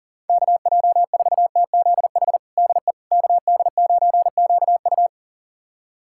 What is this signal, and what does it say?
Morse code, heard as a keyed tone, about 37 wpm, 700 Hz. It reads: KJ4TZH DE KD9QU
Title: USA Callsign DE Callsign 40wpm